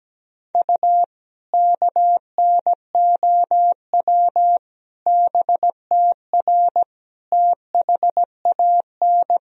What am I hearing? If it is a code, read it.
Morse code, heard as a keyed tone, about 17 wpm, 700 Hz. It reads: U KNOW BTR THAN
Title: Sentences from Top 400 Words and Abbreviations 17wpm